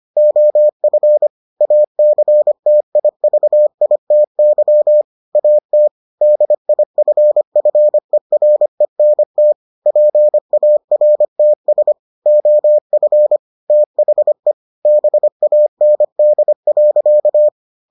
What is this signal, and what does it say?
Morse code, heard as a keyed tone, about 25 wpm, 600 Hz. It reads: OF ACTIVITY AT DIFFERENT PARTS OF THE BAND.